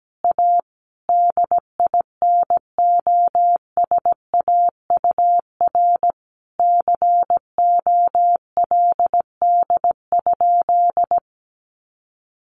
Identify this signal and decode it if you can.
Morse code, heard as a keyed tone, about 17 wpm, 700 Hz. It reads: A DINOSAUR COLD?
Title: A Sound Of Thunder 17wpm